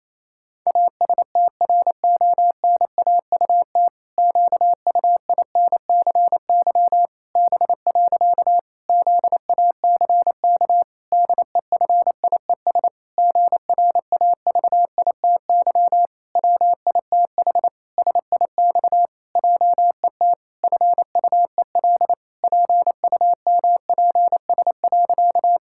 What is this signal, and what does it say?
Morse code, heard as a keyed tone, about 28 wpm, 700 Hz. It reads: ASTRONAUT QUINCY B. ZACK DEFIES GRAVITY WITH SIX JET FUEL PUMPS.
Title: Pangram Challenge v2 - 28wpm